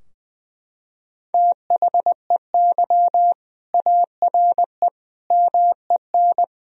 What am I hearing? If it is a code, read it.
Morse code, heard as a keyed tone, about 20 wpm, 700 Hz. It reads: THEY ARE MEN
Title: Sentences from Top 1000 Words - Review Part 2 - 20wpm